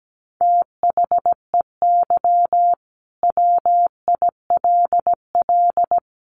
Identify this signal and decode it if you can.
Morse code, heard as a keyed tone, about 17 wpm, 700 Hz. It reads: THEY WILL
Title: Sentences from Top 1000 Words 17wpm